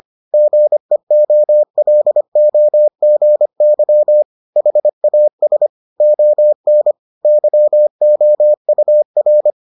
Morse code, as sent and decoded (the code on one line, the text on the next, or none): --. --- .-.. --- --. -.-- / .... .- ... / --- -. / -.-- --- ..- .-.
GEOLOGY HAS ON YOUR